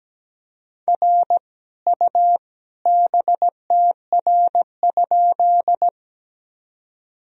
Morse code, heard as a keyed tone, about 17 wpm, 700 Hz. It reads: R U BTR?
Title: Sentences from Top 400 Words and Abbreviations 17wpm